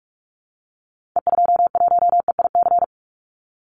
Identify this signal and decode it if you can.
Morse code, heard as a keyed tone, about 44 wpm, 700 Hz. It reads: E21EIC
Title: Call Signs Repeated 3x - 45wpm